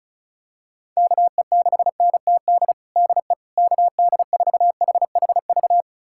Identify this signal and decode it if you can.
Morse code, heard as a keyed tone, about 36 wpm, 700 Hz. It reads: KE6NTD DE KD4HHV
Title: USA Callsign DE Callsign 35wpm